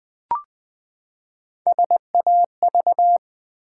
Morse code, heard as a keyed tone, about 20 wpm, 700 Hz. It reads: SAV
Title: Build A Word - 3 Letters At A Time - 6-Letter Words - 20wpm